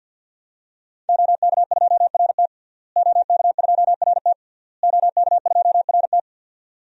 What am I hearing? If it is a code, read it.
Morse code, heard as a keyed tone, about 52 wpm, 700 Hz. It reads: OK2RT OK2RT OK2RT
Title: DXCC-100 Call Signs - 5 Characters - Repeated 3x - 50wpm